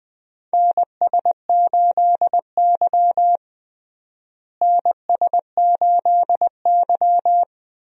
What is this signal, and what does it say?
Morse code, heard as a keyed tone, about 20 wpm, 700 Hz. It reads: NS8Y NS8Y
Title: USA Call Signs - 4 Characters - Repeated 2x - 20wpm